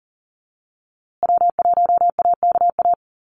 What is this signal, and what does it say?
Morse code, heard as a keyed tone, about 40 wpm, 700 Hz. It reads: W1AKA
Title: USA Call Signs Repeated 3x 40wpm